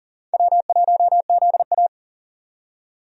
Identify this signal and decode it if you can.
Morse code, heard as a keyed tone, about 40 wpm, 700 Hz. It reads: W1ZA
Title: USA Call Signs - 4 Characters - 40wpm